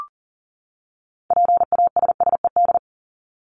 Morse code, heard as a keyed tone, about 40 wpm, 700 Hz. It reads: PASSED
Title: ED Suffix - Top 100 Words with Base Words 40wpm